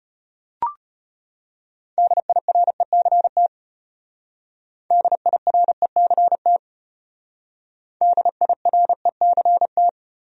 Morse code, heard as a keyed tone, about 34 wpm, 700 Hz. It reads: DIRECT DIRECT DIRECT